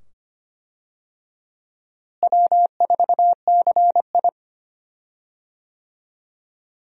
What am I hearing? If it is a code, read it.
Morse code, heard as a keyed tone, about 25 wpm, 700 Hz. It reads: W4CI
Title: USA Call Signs - 4 Characters - Repeated 2x - 25wpm